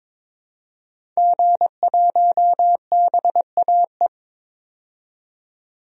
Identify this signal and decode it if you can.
Morse code, heard as a keyed tone, about 21 wpm, 700 Hz. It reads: G1BAE
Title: DXCC-100 Call Signs - 5 Characters - 22wpm